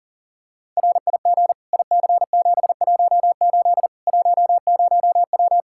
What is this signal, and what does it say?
Morse code, heard as a keyed tone, about 37 wpm, 700 Hz. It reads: RIG IC718 10W
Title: Practice QSO - Exchange 2 - Info - 40wpm